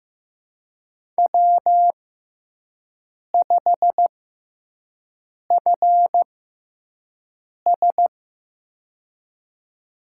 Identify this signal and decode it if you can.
Morse code, heard as a keyed tone, about 16 wpm, spaced as for 5 wpm, 700 Hz. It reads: W5FS
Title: USA Call Signs - 6x Character Spacing - 15wpm